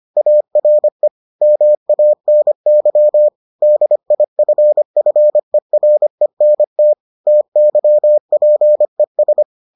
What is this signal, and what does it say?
Morse code, heard as a keyed tone, about 25 wpm, 600 Hz. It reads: ARE MANY DIFFERENT TYPES